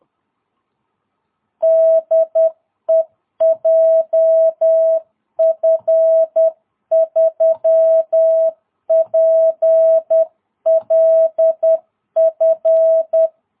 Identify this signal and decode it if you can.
Morse code, heard as a keyed tone, about 10 wpm, 660 Hz. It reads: DE JF3PLF
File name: jh8cbh / 2022cw